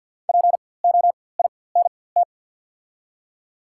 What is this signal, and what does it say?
Morse code, heard as a keyed tone, about 53 wpm, spaced as for 24 wpm, 690 Hz. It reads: POINT
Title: 4x Character Spacing - Top 300 Words - 50wpm